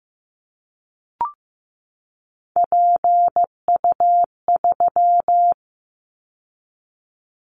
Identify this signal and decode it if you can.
Morse code, heard as a keyed tone, about 15 wpm, 700 Hz. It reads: PU3
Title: Callsign Prefixes for Top 100 DXCC - 15wpm